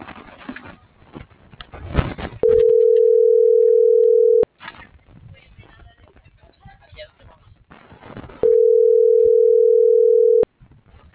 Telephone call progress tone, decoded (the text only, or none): ringback tone